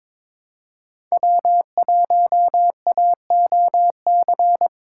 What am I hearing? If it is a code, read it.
Morse code, heard as a keyed tone, about 26 wpm, 700 Hz. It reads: W1AOC
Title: Speed Racing - USA 1x3 Call Signs 22wpm